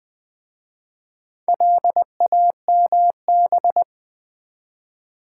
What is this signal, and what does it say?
Morse code, heard as a keyed tone, about 20 wpm, 700 Hz. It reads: LAMB
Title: Easy Words with Hints - 20wpm